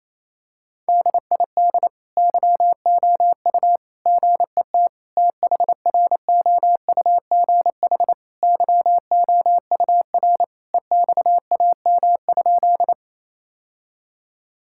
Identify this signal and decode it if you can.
Morse code, heard as a keyed tone, about 28 wpm, 700 Hz. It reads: DID YOU GET THROUGH YOUR EXAM?
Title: Sentences from Top 1000 Words - Review Part 5 - 28wpm